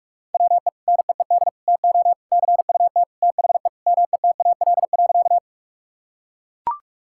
Morse code, heard as a keyed tone, about 41 wpm, 700 Hz. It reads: WE NEED TO CUT THE METAL.
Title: Sentences from Top 1000 Words - Review Part 6 - 45wpm